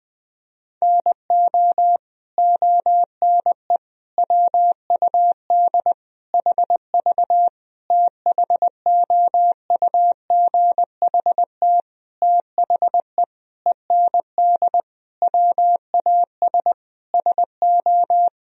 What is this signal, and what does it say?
Morse code, heard as a keyed tone, about 21 wpm, 700 Hz. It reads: NO ONE WUD HV THOUGHT THE END WAS SO
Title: Sentences from Top 300 Words and Abbreviations 20wpm